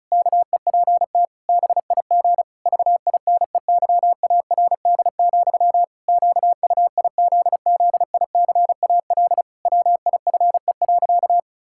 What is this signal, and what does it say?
Morse code, heard as a keyed tone, about 36 wpm, 700 Hz. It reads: KEPT BIG VINEYARD, QUIZZICAL WIFE.
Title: Pangram Challenge v2 - 35wpm